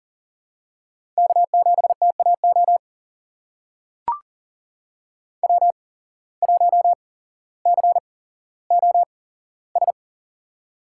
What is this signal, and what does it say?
Morse code, heard as a keyed tone, about 38 wpm, spaced as for 17 wpm, 700 Hz. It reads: K7TAO W1COS
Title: USA Call Signs - 5 Characters - 8x Character Spacing - 40wpm